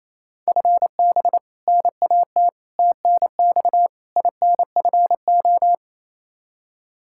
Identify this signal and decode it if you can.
Morse code, heard as a keyed tone, about 28 wpm, 700 Hz. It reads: FB NAT TNX INFO
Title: Practice QSO - Exchange 3 - More Info - 28wpm